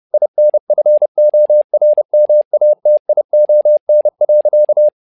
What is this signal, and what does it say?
Morse code, heard as a keyed tone, about 30 wpm, 600 Hz. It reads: INFORMATION.
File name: learning center / tower trouble / tower trouble 30wpm